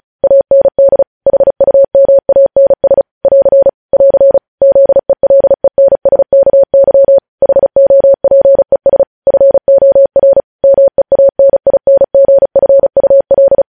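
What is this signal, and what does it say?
Morse code, heard as a keyed tone, about 36 wpm, 550 Hz. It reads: AND HUMANS <AR> <AR> ZELENSKY HOPES FOR MEANINGFUL
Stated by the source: News headlines sent in Morse code at 35 WPM.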